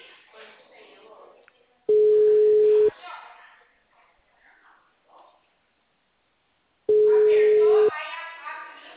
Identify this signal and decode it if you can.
Ringback tone (Europe, 425 Hz), with speech.
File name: out- -2039- - - .1847